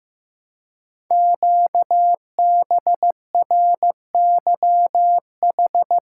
Morse code, heard as a keyed tone, about 15 wpm, 700 Hz. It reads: QBRYH
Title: Code Groups - 5 Letters - 15wpm